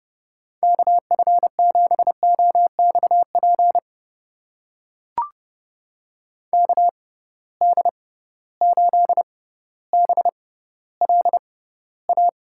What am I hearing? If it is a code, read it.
Morse code, heard as a keyed tone, about 29 wpm, spaced as for 15 wpm, 700 Hz. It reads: KF7OXP KD8BLA
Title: USA Call Signs - 6 Characters - 6x Character Spacing - 30wpm